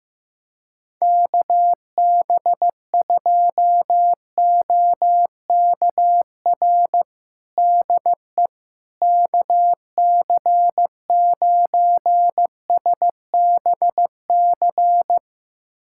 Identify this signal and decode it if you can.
Morse code, heard as a keyed tone, about 15 wpm, 700 Hz. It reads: KB2OKR DE KC9SBC
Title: USA Callsign DE Callsign - 6 Characters - 15wpm